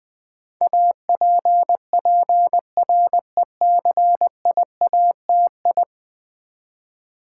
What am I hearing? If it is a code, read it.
Morse code, heard as a keyed tone, about 21 wpm, 700 Hz. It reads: APPRECIATI
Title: Build A Word - 5 Letters At A Time - 12-Letter Words - 20wpm